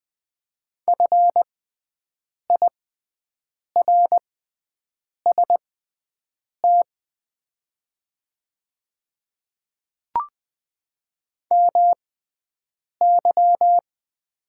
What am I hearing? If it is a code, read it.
Morse code, heard as a keyed tone, about 20 wpm, spaced as for 7 wpm, 700 Hz. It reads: FIRST MY
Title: 6x Character Spacing - Top 200 Words - 20wpm